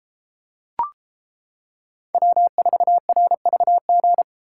Morse code, heard as a keyed tone, about 32 wpm, 700 Hz. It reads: W4RVG